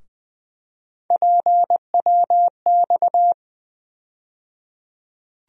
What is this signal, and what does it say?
Morse code, heard as a keyed tone, about 20 wpm, 700 Hz. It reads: PWX
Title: USA Call Sign Postfixes - 3 Letters 20wpm